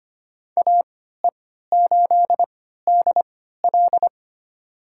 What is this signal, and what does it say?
Morse code, heard as a keyed tone, about 25 wpm, spaced as for 14 wpm, 700 Hz. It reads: AE8DL
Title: USA Call Signs - 5 Characters - 3x Character Spacing - 25wpm